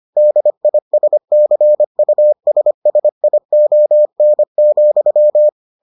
Morse code, heard as a keyed tone, about 25 wpm, 600 Hz. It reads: DISCUSSION,